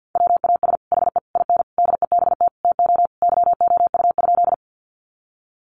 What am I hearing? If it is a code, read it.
Morse code, heard as a keyed tone, about 50 wpm, 700 Hz. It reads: WAS HE IN DEBT TO YOU?
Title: Sentences from Top 1300 Words 50wpm